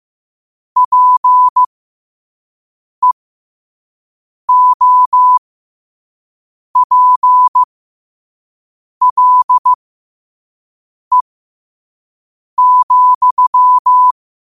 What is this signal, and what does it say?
Morse code, heard as a keyed tone, about 15 wpm, spaced as for 5 wpm, 1000 Hz. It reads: PEOPLE,
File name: qotdfiles / 5wpm